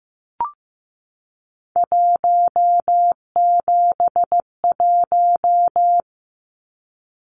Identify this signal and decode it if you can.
Morse code, heard as a keyed tone, about 15 wpm, 700 Hz. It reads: 171
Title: Numbers - 3-Digit 15wpm